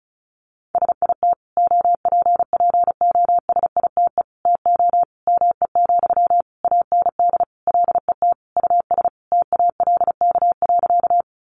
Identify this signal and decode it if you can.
Morse code, heard as a keyed tone, about 34 wpm, 700 Hz. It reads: SIT OPPOSITE TO ME, AND LET US TALK.
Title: Sentences from Top 1300 Words 35wpm